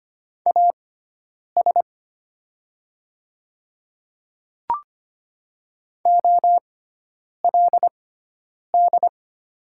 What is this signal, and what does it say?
Morse code, heard as a keyed tone, about 25 wpm, spaced as for 9 wpm, 700 Hz. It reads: AS OLD
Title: 6x Character Spacing - Top 300 Words - 25wpm